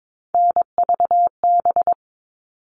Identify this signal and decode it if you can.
Morse code, heard as a keyed tone, about 22 wpm, 700 Hz. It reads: NVB
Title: All Cut Numbers - 3-Digit 22wpm